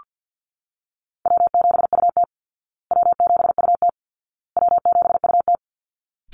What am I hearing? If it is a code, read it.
Morse code, heard as a keyed tone, about 48 wpm, 700 Hz. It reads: W7UT W7UT W7UT
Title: Easy Call Signs Repeated 3x - 50wpm